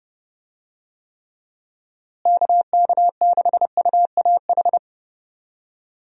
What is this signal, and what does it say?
Morse code, heard as a keyed tone, about 30 wpm, 700 Hz. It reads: KK6UAH
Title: USA Call Signs - 6 Characters - Repeated 3x - 30wpm